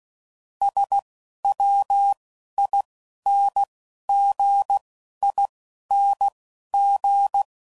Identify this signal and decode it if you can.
Morse code, heard as a keyed tone, about 16 wpm, spaced as for 11 wpm, 800 Hz. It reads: SWINGING
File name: Suffixes / Special INGs / Dble INGs